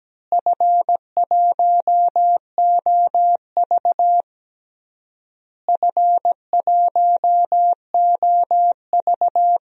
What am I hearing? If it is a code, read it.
Morse code, heard as a keyed tone, about 17 wpm, 700 Hz. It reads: F1OV F1OV
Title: DXCC-100 Call Signs - 4 Characters - Repeated 3x - 17wpm